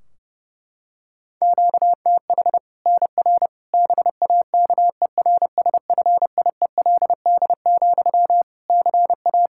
Morse code, heard as a keyed tone, about 32 wpm, 700 Hz. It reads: QTH NR BAKERSFIELD, CA
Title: Practice QSO - Exchange 1 - Beginning - 30wpm